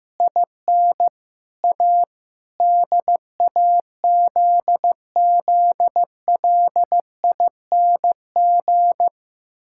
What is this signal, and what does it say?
Morse code, heard as a keyed tone, about 15 wpm, 700 Hz. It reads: IN A DAZZLING
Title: Pangram Challenge v2 - 15wpm